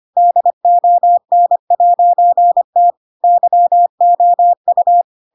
Morse code, heard as a keyed tone, about 25 wpm, 700 Hz. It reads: DON'T YOU
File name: Pinocchio / Pinocchio ch4 25 25 ik1hge